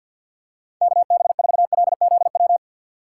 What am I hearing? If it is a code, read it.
Morse code, heard as a keyed tone, about 46 wpm, 700 Hz. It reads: KB4LZW
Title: USA Call Signs - 2x Character Spacing - 50wpm